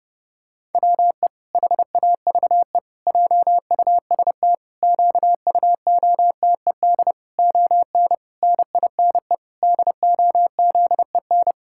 Morse code, heard as a keyed tone, about 30 wpm, 700 Hz. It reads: WE HAVE JUST QUOTED ON NINE DOZEN
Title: Pangram Challenge v2 - 30wpm